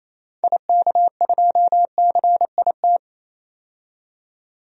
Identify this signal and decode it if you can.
Morse code, heard as a keyed tone, about 28 wpm, 700 Hz. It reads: IK2CIT
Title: DXCC-100 Call Signs - 6 Characters - 28wpm